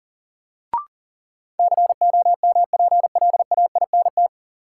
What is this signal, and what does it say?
Morse code, heard as a keyed tone, about 40 wpm, 700 Hz. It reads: COMPLAINT
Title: Sentences from Top 1400 Words - New Word Drill - 40wpm